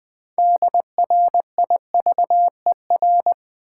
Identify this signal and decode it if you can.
Morse code, heard as a keyed tone, about 21 wpm, 700 Hz. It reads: DRIVER
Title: Sentences from Top 1000 Words - New Word Drill - 20wpm